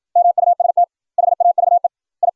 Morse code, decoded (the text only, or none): MUST HAVE I